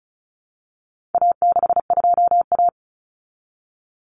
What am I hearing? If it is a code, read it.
Morse code, heard as a keyed tone, about 34 wpm, 700 Hz. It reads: A62A
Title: Common Cut Numbers - 4-Digit 35wpm